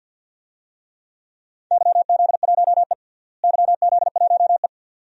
Morse code, heard as a keyed tone, about 52 wpm, 700 Hz. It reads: YZ1E YZ1E
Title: DXCC-100 Call Signs - 4 Characters - Repeated 2x - 50wpm